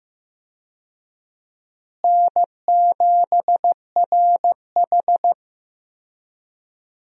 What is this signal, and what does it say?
Morse code, heard as a keyed tone, about 15 wpm, 700 Hz. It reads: N7RH
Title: USA Call Signs - 4 Characters - Repeated 3x - 15wpm